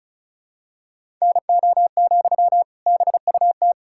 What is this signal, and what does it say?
Morse code, heard as a keyed tone, about 36 wpm, 700 Hz. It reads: NO, BUT
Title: Sentences from Top 1000 Words - Review Part 6 - 35wpm